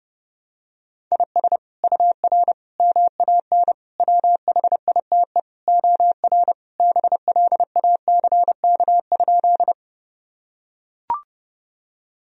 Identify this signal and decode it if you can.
Morse code, heard as a keyed tone, about 32 wpm, 700 Hz. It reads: IS UR MAN WHITE OR BLACK?
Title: Sentences from Top 400 Words and Abbreviations 30wpm